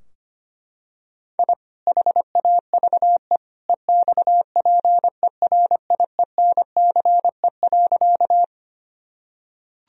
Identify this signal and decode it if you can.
Morse code, heard as a keyed tone, about 25 wpm, 700 Hz. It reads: I HAVE EXPERIENCE.
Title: Sentences from Top 1000 Words - Review Part 3 - 25wpm